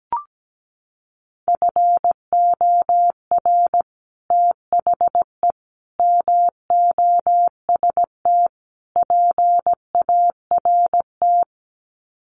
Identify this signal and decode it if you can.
Morse code, heard as a keyed tone, about 17 wpm, 700 Hz. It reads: FOR THE MOST PART
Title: Sets of 4 Words 17wpm